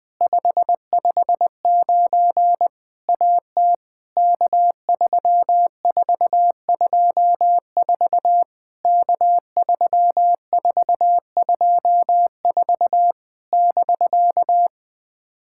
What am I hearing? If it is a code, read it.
Morse code, heard as a keyed tone, about 20 wpm, 700 Hz. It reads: 559 AT K3424 K3424 <BK>
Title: POTA QSO Practice - 20wpm